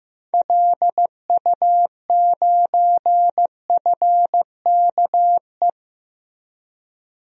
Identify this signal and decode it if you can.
Morse code, heard as a keyed tone, about 15 wpm, 700 Hz. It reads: LU9FKE
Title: DXCC-100 Call Signs - 6 Characters - Repeated 3x - 15wpm